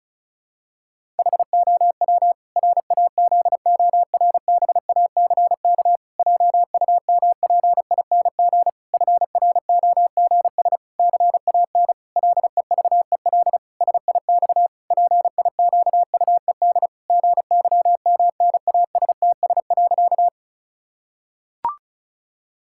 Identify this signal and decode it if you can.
Morse code, heard as a keyed tone, about 36 wpm, 700 Hz. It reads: HOW RAZORBACK JUMPING FROGS CAN LEVEL SIX PIQUED GYMNASTS.
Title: Pangram Challenge v2 - 35wpm